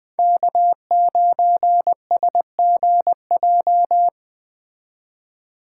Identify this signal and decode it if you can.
Morse code, heard as a keyed tone, about 21 wpm, 700 Hz. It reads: K9SGJ